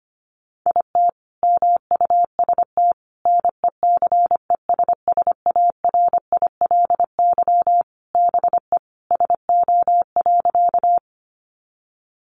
Morse code, heard as a keyed tone, about 25 wpm, 700 Hz. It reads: IT MUST NECESSARILY BE SO.
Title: Sentences from Top 1400 Words 25wpm